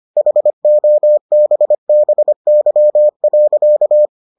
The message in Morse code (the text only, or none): HOBBY.